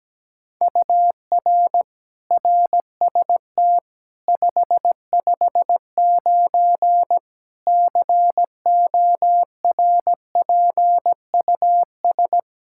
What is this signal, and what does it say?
Morse code, heard as a keyed tone, about 17 wpm, 700 Hz. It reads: UR RST 559 CORPUS
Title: Practice QSO - Exchange 1 - Beginning - 17wpm